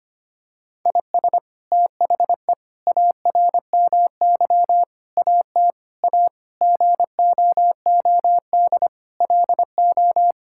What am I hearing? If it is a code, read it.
Morse code, heard as a keyed tone, about 25 wpm, 700 Hz. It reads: IS THE ARMY AT A GOOD LO
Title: Sentences from Top 1000 Words - Review Part 6 - 25wpm